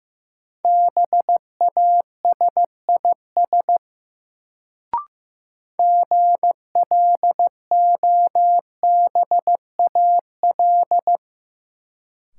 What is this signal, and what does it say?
Morse code, heard as a keyed tone, about 15 wpm, 700 Hz. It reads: BASIS GLOBAL
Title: Sentences from Top 1200 Words - New Word Drill Alt - 15wpm